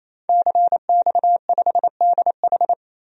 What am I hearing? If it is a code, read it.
Morse code, heard as a keyed tone, about 28 wpm, 700 Hz. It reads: CX5DH
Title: DXCC-100 Call Signs - 5 Characters - 28wpm